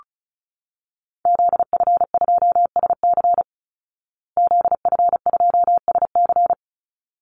Morse code, heard as a keyed tone, about 34 wpm, 700 Hz. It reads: ZF2SC ZF2SC
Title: Call Signs Repeated 3x - 35wpm